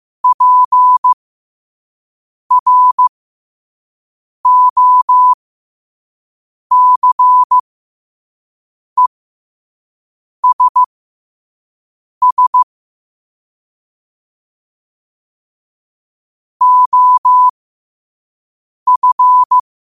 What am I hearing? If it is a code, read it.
Morse code, heard as a keyed tone, about 15 wpm, spaced as for 5 wpm, 1000 Hz. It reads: PROCESS OF
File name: qotdfiles / 5wpm